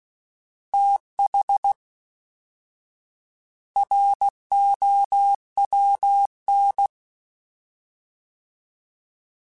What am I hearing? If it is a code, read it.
Morse code, heard as a keyed tone, about 16 wpm, 800 Hz. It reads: TH ROWN